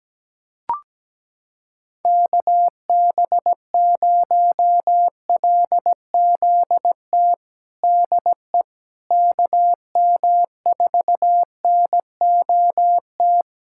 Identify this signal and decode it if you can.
Morse code, heard as a keyed tone, about 17 wpm, 700 Hz. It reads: KB0LZT DE KM4NOT
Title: USA Callsign DE Callsign - 6 Characters - 17wpm